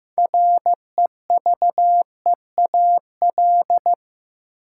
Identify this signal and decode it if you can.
Morse code, heard as a keyed tone, about 15 wpm, 700 Hz. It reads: REVEAL
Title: Six Letter Words - 15wpm